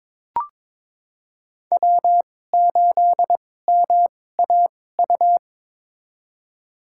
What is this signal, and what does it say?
Morse code, heard as a keyed tone, about 23 wpm, spaced as for 16 wpm, 700 Hz. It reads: W8MAU
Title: USA Call Signs - 5 Characters - 2x Character Spacing - 22wpm